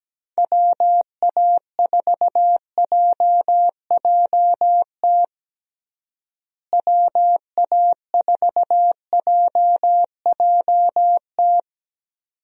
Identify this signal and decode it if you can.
Morse code, heard as a keyed tone, about 17 wpm, 700 Hz. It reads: WA4JJT WA4JJT
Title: USA Call Signs - 6 Characters - Repeated 2x - 17wpm